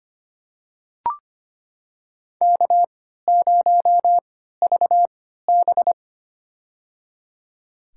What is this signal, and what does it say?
Morse code, heard as a keyed tone, about 25 wpm, spaced as for 14 wpm, 700 Hz. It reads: K0VB
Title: USA Call Signs - 4 Characters - 3x Character Spacing - 25wpm